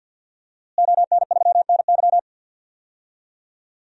Morse code, heard as a keyed tone, about 53 wpm, 700 Hz. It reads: ON3NY
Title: DXCC-100 Call Signs - 5 Characters - Repeated 3x - 50wpm